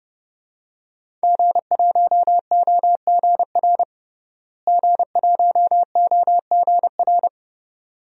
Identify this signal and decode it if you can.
Morse code, heard as a keyed tone, about 30 wpm, 700 Hz. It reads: G1OGR G1OGR
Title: DXCC-100 Call Signs - 5 Characters - Repeated 2x - 30wpm